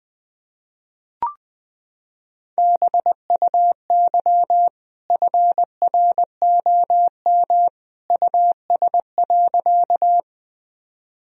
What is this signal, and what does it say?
Morse code, heard as a keyed tone, about 20 wpm, 700 Hz. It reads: BUY FROM US.
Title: Sentences from Top 1000 Words - Review Part 4 - 20wpm